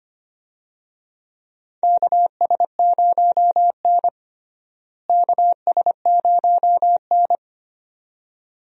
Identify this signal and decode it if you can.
Morse code, heard as a keyed tone, about 25 wpm, 700 Hz. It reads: KS0N KS0N
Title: USA Call Signs - 4 Characters - Repeated 2x - 25wpm